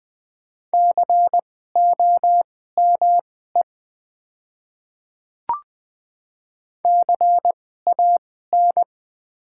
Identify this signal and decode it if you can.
Morse code, heard as a keyed tone, about 20 wpm, spaced as for 14 wpm, 700 Hz. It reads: COME CAN
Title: 2x Character Spacing - Top 100 Words - 20wpm